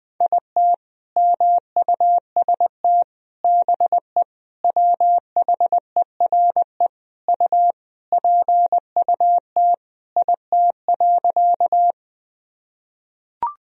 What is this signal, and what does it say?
Morse code, heard as a keyed tone, about 20 wpm, 700 Hz. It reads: IT MUST BE WHERE U PUT IT.
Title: Sentences from Top 200 Words and Abbreviations 20wpm